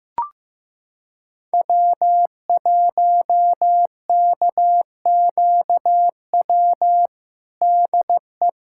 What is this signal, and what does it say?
Morse code, heard as a keyed tone, about 16 wpm, 700 Hz. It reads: W1KQW DE
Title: USA Callsign DE Callsign - 5 Characters - 15wpm